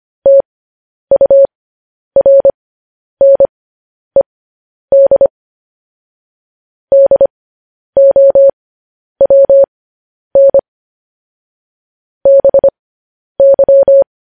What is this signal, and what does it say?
Morse code, heard as a keyed tone, about 25 wpm, spaced as for 10 wpm, 550 Hz. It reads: TURNED DOWN BY
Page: news headlines in morse code at 10 wpm